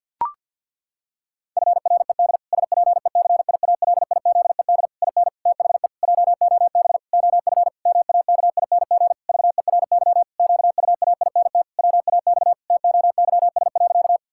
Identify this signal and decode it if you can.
Morse code, heard as a keyed tone, about 50 wpm, 700 Hz. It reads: FRED SPECIALIZED IN THE JOB OF MAKING VERY QUAINT WAX TOYS.
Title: Pangram Challenge v2 - 50wpm